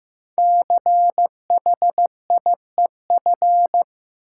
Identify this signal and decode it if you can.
Morse code, heard as a keyed tone, about 15 wpm, 700 Hz. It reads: CHIEF
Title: Sets of 5 Words Encore - New Word Drill - 15wpm